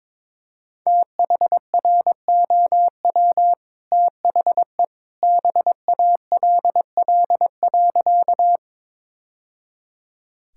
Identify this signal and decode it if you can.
Morse code, heard as a keyed tone, about 22 wpm, 690 Hz. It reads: THROW THE BALL.
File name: practice / 22wpm / Easy Sentences with Hints - 22wpm